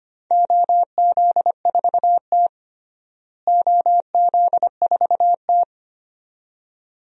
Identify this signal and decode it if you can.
Morse code, heard as a keyed tone, about 25 wpm, 700 Hz. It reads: OZ4T OZ4T
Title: DXCC-100 Call Signs - 4 Characters - Repeated 2x - 25wpm